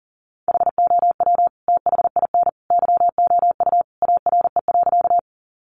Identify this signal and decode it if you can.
Morse code, heard as a keyed tone, about 37 wpm, 700 Hz. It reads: HOW THIN YOU ARE.
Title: Sentences from Top 1200 Words 40wpm